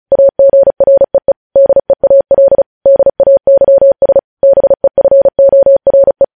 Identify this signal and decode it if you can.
Morse code, heard as a keyed tone, about 36 wpm, 550 Hz. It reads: AGREE DEAL DAYS BEFORE
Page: news headlines in morse code at 35 wpm